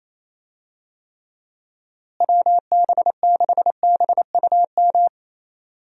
Morse code, heard as a keyed tone, about 28 wpm, 700 Hz. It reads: WB6BUM
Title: USA Call Signs - 6 Characters - Repeated 2x - 28wpm